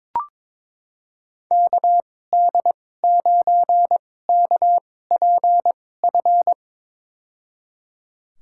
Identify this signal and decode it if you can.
Morse code, heard as a keyed tone, about 21 wpm, spaced as for 16 wpm, 700 Hz. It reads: KD9KPF
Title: USA Call Signs - 6 Characters - 2x Character Spacing - 22wpm